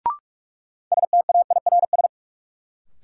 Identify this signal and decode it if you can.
Morse code, heard as a keyed tone, about 46 wpm, 700 Hz. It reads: STAIRS
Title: Sets of 5 Words Encore - New Word Drill - 45wpm